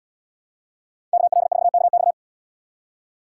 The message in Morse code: KL4KX

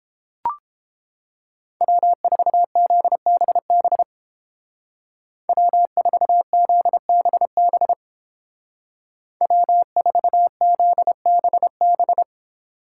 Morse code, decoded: W4ZBB W4ZBB W4ZBB

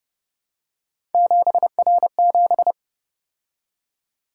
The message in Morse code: 7R7

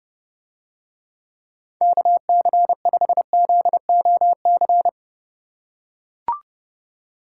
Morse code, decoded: KC5ZOC

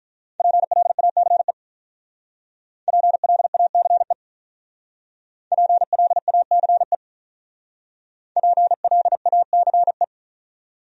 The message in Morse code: PLACE PLACE PLACE PLACE

